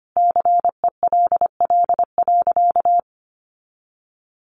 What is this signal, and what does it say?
Morse code, heard as a keyed tone, about 25 wpm, 700 Hz. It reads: CELL.
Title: Sentences from Top 1100 Words 25wpm